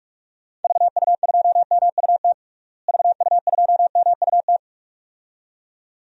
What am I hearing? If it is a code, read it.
Morse code, heard as a keyed tone, about 46 wpm, 700 Hz. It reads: VU2MUT VU2MUT
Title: DXCC-100 Call Signs - 6 Characters - Repeated 2x - 45wpm